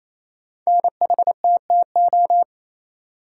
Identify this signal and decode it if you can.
Morse code, heard as a keyed tone, about 27 wpm, 700 Hz. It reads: NHTTO
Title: Code Groups - 5 Letters - 28wpm